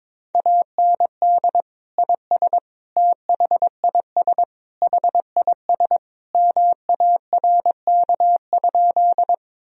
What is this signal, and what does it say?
Morse code, heard as a keyed tone, about 22 wpm, 700 Hz. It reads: AND IS THIS HIS MARK?
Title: Sentences from Top 1000 Words - Review Part 1 - 22wpm